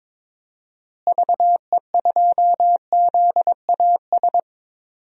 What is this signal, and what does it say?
Morse code, heard as a keyed tone, about 22 wpm, 700 Hz. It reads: VE2ZAS
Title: DXCC-100 Call Signs - 6 Characters - Repeated 2x - 22wpm